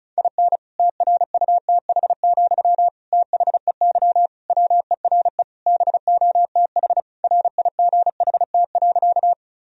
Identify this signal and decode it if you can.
Morse code, heard as a keyed tone, about 34 wpm, 700 Hz. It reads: IN TRUTH, THEY WERE BOTH RIGHT.
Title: Sentences from Top 1000 Words - Review Part 1 - 35wpm